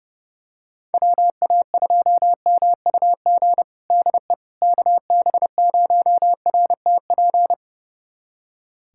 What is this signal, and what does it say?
Morse code, heard as a keyed tone, about 32 wpm, 700 Hz. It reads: WA2MUG DE KB0RTP
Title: USA Callsign DE Callsign - 6 Characters - 30wpm